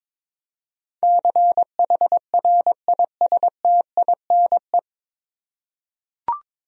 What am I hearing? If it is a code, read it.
Morse code, heard as a keyed tone, about 22 wpm, 700 Hz. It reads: CHRISTINE
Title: USA Common Names - 22wpm